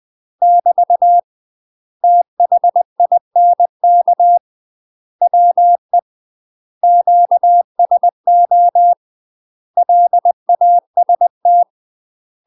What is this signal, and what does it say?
Morse code, heard as a keyed tone, about 20 wpm, 700 Hz. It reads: <BT> THINK WE QSO LAST
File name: UK MASTER QSOs TRAINING / 20WPM / UK MASTER QSOs 0051